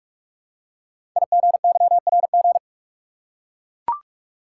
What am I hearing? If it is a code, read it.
Morse code, heard as a keyed tone, about 44 wpm, 700 Hz. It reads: IGYRG
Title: Code Groups - 5 Letters - 45wpm